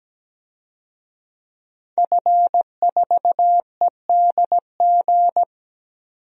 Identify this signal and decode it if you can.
Morse code, heard as a keyed tone, about 17 wpm, 700 Hz. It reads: F4EDG
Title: DXCC-100 Call Signs - 5 Characters - 17wpm